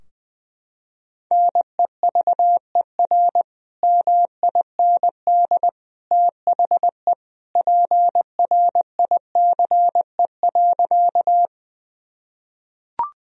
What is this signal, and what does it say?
Morse code, heard as a keyed tone, about 20 wpm, 700 Hz. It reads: NEVER MIND THE PRICE.
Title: Sentences from Top 1000 Words - Review Part 4 - 20wpm